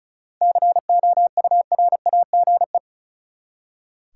Morse code, heard as a keyed tone, about 36 wpm, 700 Hz. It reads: COURAGE
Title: Seven Letter Words - 35wpm